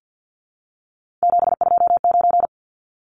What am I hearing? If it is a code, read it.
Morse code, heard as a keyed tone, about 50 wpm, 700 Hz. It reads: 729
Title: Numbers - 3-Digit 50wpm